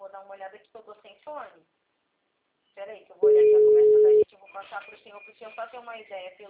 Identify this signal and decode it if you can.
Ringback tone (Europe, 425 Hz), with speech.